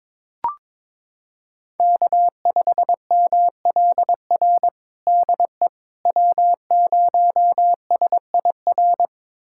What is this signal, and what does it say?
Morse code, heard as a keyed tone, about 22 wpm, 700 Hz. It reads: K5MLR DE W0SIR
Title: USA Callsign DE Callsign - 5 Characters - 22wpm